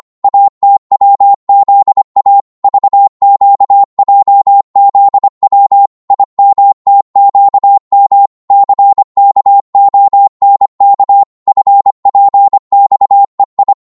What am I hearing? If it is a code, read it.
Morse code, heard as a keyed tone, about 25 wpm, 800 Hz. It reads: ATWZA VQJZW IMTQM CKONK FPXEI